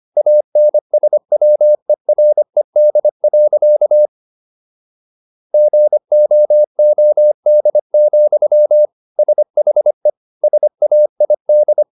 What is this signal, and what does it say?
Morse code, heard as a keyed tone, about 25 wpm, 600 Hz. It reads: ANSWERED. GOOD, SHE SAID